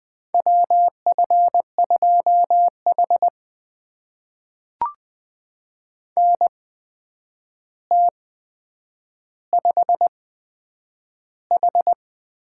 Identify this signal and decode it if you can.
Morse code, heard as a keyed tone, about 20 wpm, spaced as for 9 wpm, 700 Hz. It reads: WF2H NT5H
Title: USA Call Signs - 4 Characters - 8x Character Spacing - 20wpm